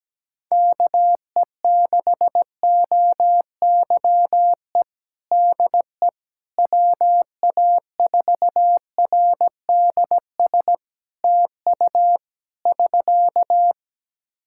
Morse code, heard as a keyed tone, about 17 wpm, 700 Hz. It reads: KE6OYE DE WA4RDS TU <SK>
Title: Practice QSO - Exchange 4 - Ending - 17wpm